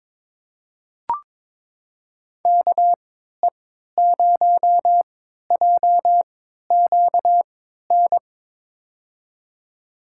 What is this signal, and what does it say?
Morse code, heard as a keyed tone, about 22 wpm, spaced as for 13 wpm, 700 Hz. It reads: KE0JQN
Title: USA Call Signs - 6 Characters - 3x Character Spacing - 22wpm